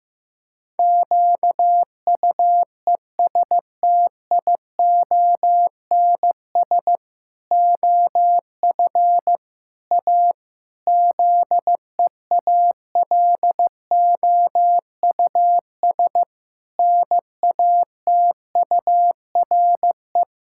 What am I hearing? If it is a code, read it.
Morse code, heard as a keyed tone, about 15 wpm, 700 Hz. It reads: QUESTIONS OF A ZEALOUS NATURE
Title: Pangram Challenge v2 - 15wpm